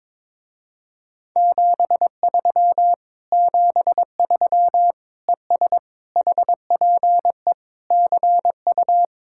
Morse code, heard as a keyed tone, about 22 wpm, 700 Hz. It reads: 73 73 ES HPE CU
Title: Speed-Racing Common QSO Phrases - 22wpm